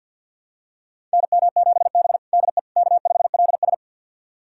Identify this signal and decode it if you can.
Morse code, heard as a keyed tone, about 50 wpm, 700 Hz. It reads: NM7B DE K5LS
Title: USA Callsign DE Callsign - 4 Characters - 50wpm